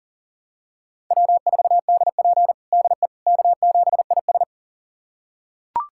Morse code, heard as a keyed tone, about 40 wpm, 700 Hz. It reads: W4DP DE K7IS
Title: USA Callsign DE Callsign - 4 Characters - 40wpm